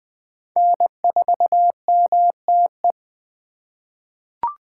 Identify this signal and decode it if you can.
Morse code, heard as a keyed tone, about 20 wpm, 700 Hz. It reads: N4MTE
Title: 2023 POTA Activator Call Signs - 20wpm